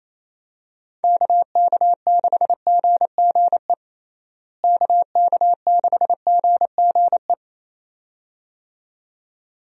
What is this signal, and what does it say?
Morse code, heard as a keyed tone, about 28 wpm, 700 Hz. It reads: KK6GGE KK6GGE
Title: USA Call Signs - 6 Characters - Repeated 2x - 28wpm